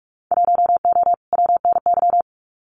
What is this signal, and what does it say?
Morse code, heard as a keyed tone, about 44 wpm, 700 Hz. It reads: 1O WNY
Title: Winter Field Day Exchanges 45wpm